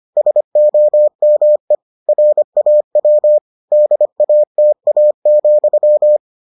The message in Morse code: SOME RAW DATA,

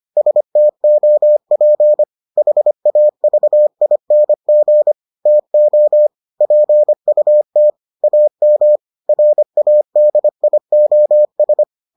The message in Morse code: STOP HAVING TO PUT AM RADIOS

